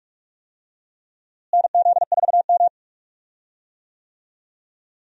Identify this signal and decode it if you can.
Morse code, heard as a keyed tone, about 46 wpm, 700 Hz. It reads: NZ4M
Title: USA Call Signs - 4 Characters - Repeated 2x - 45wpm